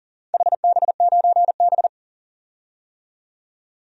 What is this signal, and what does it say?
Morse code, heard as a keyed tone, about 40 wpm, 700 Hz. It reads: HB9B
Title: DXCC-100 Call Signs - 4 Characters - 40wpm